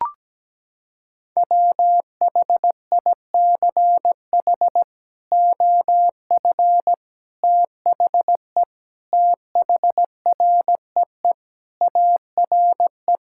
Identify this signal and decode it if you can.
Morse code, heard as a keyed tone, about 17 wpm, 700 Hz. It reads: WHICH OF THE THREE ARE
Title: Sentences from Top 1000 Words - Review Part 5 - 17wpm